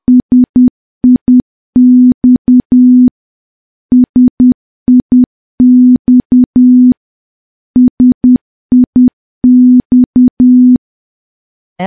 Morse code, heard as a keyed tone, about 10 wpm, 250 Hz. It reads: SIX SIX SIX
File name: morse / morse-training-10WPM